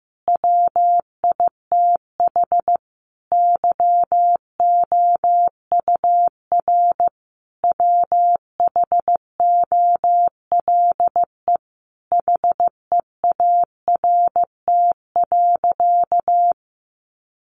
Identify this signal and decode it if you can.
Morse code, heard as a keyed tone, about 15 wpm, 700 Hz. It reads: WITH YOUR WHOLE HEART.
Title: Sentences from Top 1200 Words 15wpm